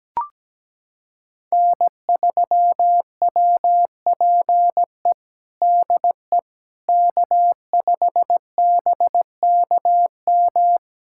Morse code, heard as a keyed tone, about 17 wpm, 700 Hz. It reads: N3WPE DE K5BKM
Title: USA Callsign DE Callsign - 5 Characters - 17wpm